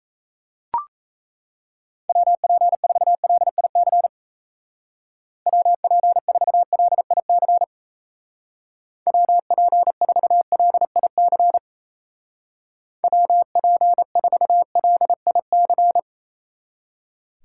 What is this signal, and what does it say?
Morse code, heard as a keyed tone, about 34 wpm, 700 Hz. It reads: WP4LIC WP4LIC WP4LIC WP4LIC
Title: Speed Racing - USA 2x3 Call Signs 28wpm